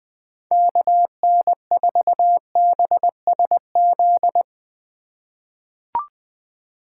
Morse code, heard as a keyed tone, about 20 wpm, 700 Hz. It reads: KN4BSZ
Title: USA Call Signs - 6 Characters - 2x Character Spacing - 20wpm